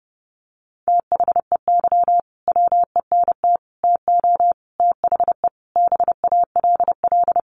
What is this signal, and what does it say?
Morse code, heard as a keyed tone, about 30 wpm, 700 Hz. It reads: THEY WENT TO THE BALL
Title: Sentences from Top 900 Words 30wpm